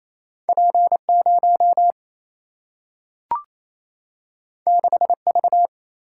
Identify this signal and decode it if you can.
Morse code, heard as a keyed tone, about 28 wpm, 700 Hz. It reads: P0 6V
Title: Single-Letter Single-Number 28wpm